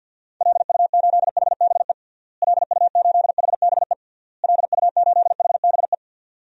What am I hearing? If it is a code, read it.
Morse code, heard as a keyed tone, about 47 wpm, 700 Hz. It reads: LU8HBE LU8HBE LU8HBE
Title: DXCC-100 Call Signs - 6 Characters - Repeated 3x - 50wpm